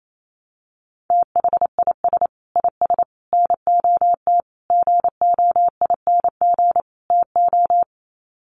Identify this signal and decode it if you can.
Morse code, heard as a keyed tone, about 28 wpm, 700 Hz. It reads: THIS IS NOT GOING TO
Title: Sets of 5 Words Encore2 28wpm